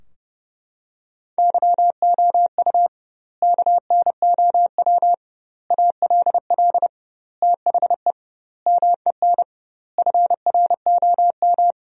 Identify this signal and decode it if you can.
Morse code, heard as a keyed tone, about 30 wpm, 700 Hz. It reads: YOU KNOW ALL THE MEN FROM
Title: 2x Word-Spacing - Sentences from Top 300 Words - 30wpm